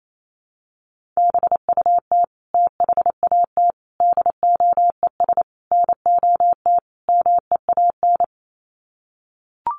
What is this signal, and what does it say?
Morse code, heard as a keyed tone, about 28 wpm, 700 Hz. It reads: BUT THAT DOES NOT MEAN
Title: Sets of 5 Words 28wpm